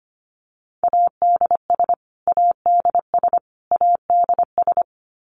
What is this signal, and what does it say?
Morse code, heard as a keyed tone, about 25 wpm, 700 Hz. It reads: ADS ADS ADS
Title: Abbreviations Repeated 3x - 25wpm